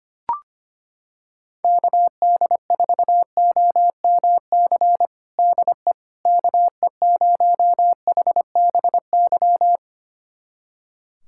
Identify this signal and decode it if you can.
Morse code, heard as a keyed tone, about 25 wpm, 700 Hz. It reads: KD4OMC DE KE0HBY
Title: USA Callsign DE Callsign - 6 Characters - 25wpm